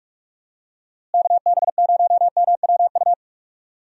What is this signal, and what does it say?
Morse code, heard as a keyed tone, about 44 wpm, 700 Hz. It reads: KB0MWU
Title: USA Call Signs - 6 Characters - Repeated 3x - 45wpm